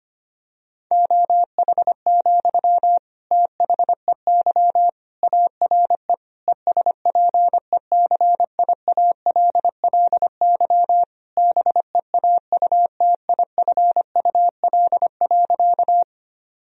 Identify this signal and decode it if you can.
Morse code, heard as a keyed tone, about 25 wpm, 700 Hz. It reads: OH, THEY ARE ESPECIALLY BEAUTIFUL.
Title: Sentences from Top 1000 Words - Review Part 4 - 25wpm